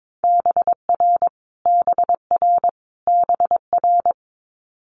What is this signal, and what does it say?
Morse code, heard as a keyed tone, about 22 wpm, 700 Hz. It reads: BR BR BR
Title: Multi-letter Phonemes Repeated 3x 22wpm